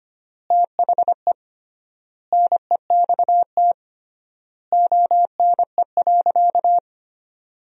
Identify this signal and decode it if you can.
Morse code, heard as a keyed tone, about 25 wpm, 700 Hz. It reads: THE NEXT ONE.
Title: 3x Word-Spacing - Sentences from Top 400 Words - 25wpm